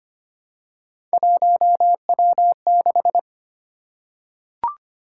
Morse code, heard as a keyed tone, about 25 wpm, 700 Hz. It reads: 1W6
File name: Three Letter-Number Combinations - 25wpm